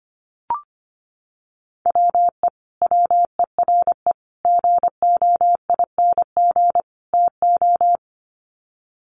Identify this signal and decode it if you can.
Morse code, heard as a keyed tone, about 25 wpm, 700 Hz. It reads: WE WERE GOING TO
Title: Sets of 4 Words 25wpm